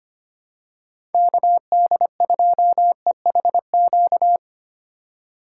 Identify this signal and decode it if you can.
Morse code, heard as a keyed tone, about 25 wpm, 700 Hz. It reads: KD2EHQ
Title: USA Call Signs - 6 Characters - 8x Character Spacing - 25wpm